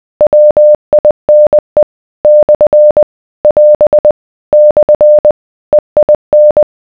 Morse code, heard as a keyed tone, about 20 wpm, 600 Hz. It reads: WINE / <AS> / EIN